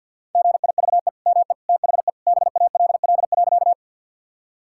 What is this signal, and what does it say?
Morse code, heard as a keyed tone, about 50 wpm, 700 Hz. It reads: GIVE ME THE BALL.
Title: Sentences from Top 1000 Words - Review Part 6 - 50wpm